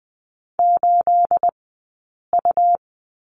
Morse code, heard as a keyed tone, about 20 wpm, spaced as for 8 wpm, 700 Hz. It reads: 8U
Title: 211 Lesson 34d - Number 8 - Two Characters 20wpm